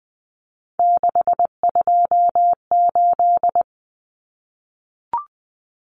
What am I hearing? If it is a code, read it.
Morse code, heard as a keyed tone, about 20 wpm, 700 Hz. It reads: 628
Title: Numbers - 3-Digit 20wpm